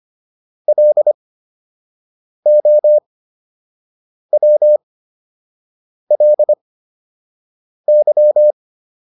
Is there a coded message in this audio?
Yes: Morse code from 0.7 to 8.5 s, heard as a keyed tone, about 25 wpm, spaced as for 6 wpm, 600 Hz. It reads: LOWLY